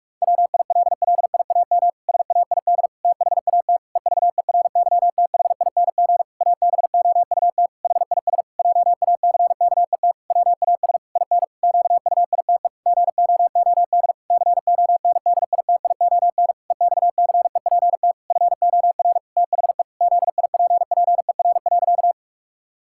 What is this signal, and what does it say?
Morse code, heard as a keyed tone, about 43 wpm, 700 Hz. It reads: WILLIAM SAID THAT EVERYTHING ABOUT HIS JACKET WAS IN QUITE GOOD CONDITION EXCEPT FOR THE ZIPPER.
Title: Pangram Challenge v2 - 45wpm